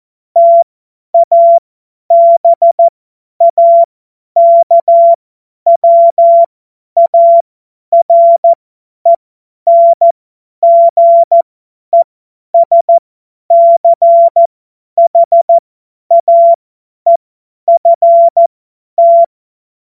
Morse code, heard as a keyed tone, about 14 wpm, spaced as for 10 wpm, 680 Hz. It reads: TABAKWARENGESCHAEFT
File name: Morsekurs / Witze / Witz 09 April 2024 50 F70